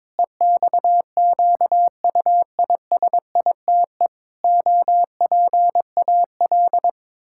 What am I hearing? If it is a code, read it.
Morse code, heard as a keyed tone, about 22 wpm, 700 Hz. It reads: EXQUISITE OPAL
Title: Pangram Challenge v2 - 22wpm